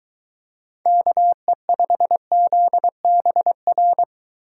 Morse code, heard as a keyed tone, about 23 wpm, 700 Hz. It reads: KE5ZBR